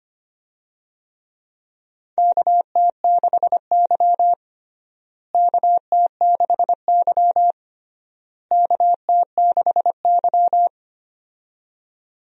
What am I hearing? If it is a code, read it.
Morse code, heard as a keyed tone, about 25 wpm, 700 Hz. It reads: KT6Y KT6Y KT6Y
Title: USA Call Signs - 4 Characters - Repeated 3x - 25wpm